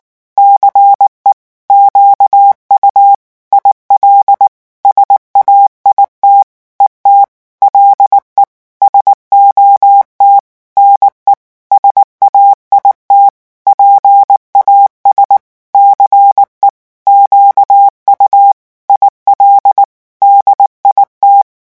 Morse code, heard as a keyed tone, about 19 wpm, 800 Hz. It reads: CE QU IL SAIT ET LE SOT NE SAIT PAS CE QU IL DIT
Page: REVISION 8 - 8 mots minute